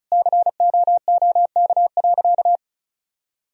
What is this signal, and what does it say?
Morse code, heard as a keyed tone, about 36 wpm, 700 Hz. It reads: COOK.
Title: Sentences from Top 1000 Words - Review Part 4 - 35wpm